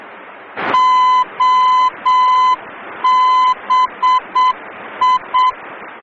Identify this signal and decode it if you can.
Morse code, heard as a keyed tone, about 7 wpm, 1010 Hz. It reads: OBI
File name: germany / mx / OBI-D